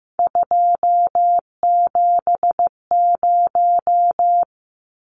Morse code, heard as a keyed tone, about 15 wpm, 700 Hz. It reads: 270
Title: Numbers - 3-Digit 15wpm